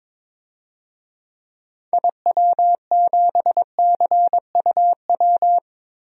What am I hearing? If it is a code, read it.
Morse code, heard as a keyed tone, about 22 wpm, 700 Hz. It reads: IW7CUW
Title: DXCC-100 Call Signs - 6 Characters - 22wpm